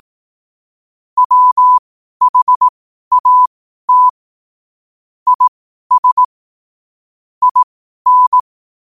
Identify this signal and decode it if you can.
Morse code, heard as a keyed tone, about 18 wpm, spaced as for 12 wpm, 1000 Hz. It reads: WHAT IS IN
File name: qotdfiles / 12wpm